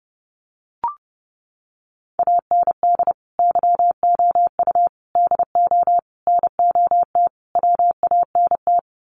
Morse code, heard as a keyed tone, about 30 wpm, 700 Hz. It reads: AND YOU DO NOT WANT
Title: Sets of 5 Words Encore2 30wpm